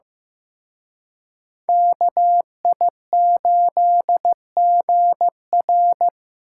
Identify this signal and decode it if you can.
Morse code, heard as a keyed tone, about 18 wpm, 700 Hz. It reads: KI8GR
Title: Speed Racing - USA 2x2 Call Signs 15wpm